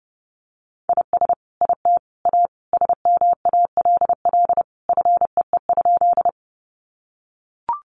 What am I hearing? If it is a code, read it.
Morse code, heard as a keyed tone, about 30 wpm, 700 Hz. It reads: IS IT A SMALL FEE?
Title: Sentences from Top 1500 Words 30wpm